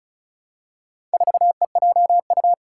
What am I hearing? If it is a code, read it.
Morse code, heard as a keyed tone, about 36 wpm, 700 Hz. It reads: 4EJU
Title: Four Challenging Letter-Number Combinations 35wpm